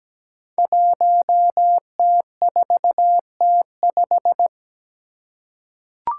Morse code, heard as a keyed tone, about 17 wpm, 700 Hz. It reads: 1T4T5
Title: Five Challenging Letter-Number Combinations 17wpm